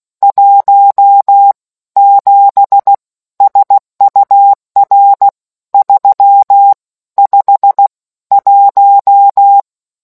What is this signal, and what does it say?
Morse code, heard as a keyed tone, about 16 wpm, spaced as for 13 wpm, 800 Hz. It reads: 17 SUR 351